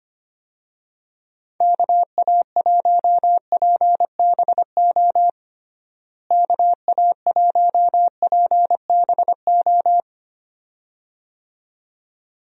Morse code, heard as a keyed tone, about 25 wpm, 700 Hz. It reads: KA1PBO KA1PBO
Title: USA Call Signs - 6 Characters - Repeated 2x - 25wpm